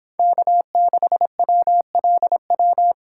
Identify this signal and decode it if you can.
Morse code, heard as a keyed tone, about 26 wpm, 700 Hz. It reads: K6WLW